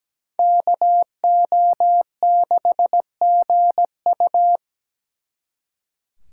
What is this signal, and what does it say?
Morse code, heard as a keyed tone, about 20 wpm, 700 Hz. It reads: KO6GU
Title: Speed Racing - USA 2x2 Call Signs 17wpm